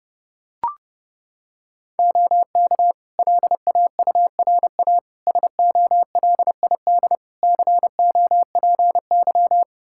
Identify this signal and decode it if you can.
Morse code, heard as a keyed tone, about 30 wpm, 700 Hz. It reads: OK LAURA SOLID COPY
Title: K1USN SST QSO Practice 30wpm